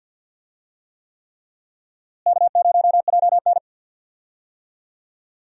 Morse code, heard as a keyed tone, about 53 wpm, 700 Hz. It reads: K0JN
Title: USA Call Signs - 4 Characters - Repeated 3x - 50wpm